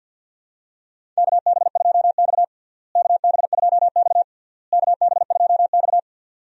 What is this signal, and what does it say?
Morse code, heard as a keyed tone, about 46 wpm, 700 Hz. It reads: KB2X KB2X KB2X
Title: USA Call Signs - 4 Characters - Repeated 3x - 50wpm